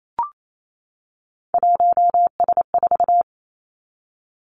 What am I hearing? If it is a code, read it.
Morse code, heard as a keyed tone, about 28 wpm, 700 Hz. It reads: 1S4
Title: Three Challenging Letter-Number Combinations 28wpm